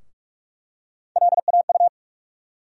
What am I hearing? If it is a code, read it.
Morse code, heard as a keyed tone, about 43 wpm, 700 Hz. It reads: LAU
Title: USA Call Sign Postfixes - 3 Letters 45wpm